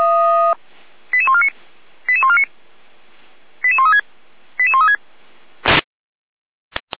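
Five-tone selective call (ZVEI), heard as quick runs of five tones, tones of about 70 ms. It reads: 88149 88149 88136 88136
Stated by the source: ZVEI five-tone selective call received by SDR, recorded on the audio server of Feuerwehr Orlamünde (f85360).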